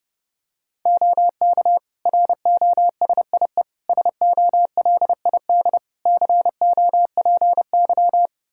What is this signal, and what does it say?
Morse code, heard as a keyed tone, about 31 wpm, 700 Hz. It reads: OK ROSIE SOLID COPY
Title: Practice QSO - Exchange 3 - More Info - 30wpm